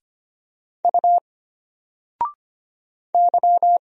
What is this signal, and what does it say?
Morse code, heard as a keyed tone, about 25 wpm, 700 Hz. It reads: U Y
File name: Single-Letter-Number-25wpm